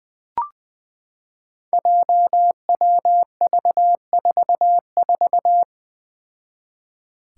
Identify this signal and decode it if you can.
Morse code, heard as a keyed tone, about 21 wpm, 700 Hz. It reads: JWV44
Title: Five Challenging Letter-Number Combinations 20wpm